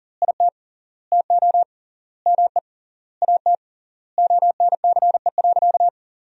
Morse code, heard as a keyed tone, about 37 wpm, 700 Hz. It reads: IT TO ME AT ONCE.
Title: 3x Word-Spacing - Sentences from Top 500 Words - 40wpm